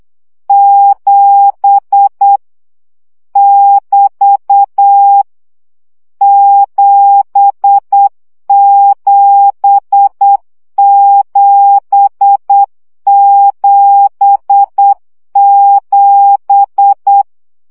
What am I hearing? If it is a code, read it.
Morse code, heard as a keyed tone, about 8 wpm, 800 Hz. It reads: 7 <BT> 77777
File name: CW gyakorlas 7 7 baud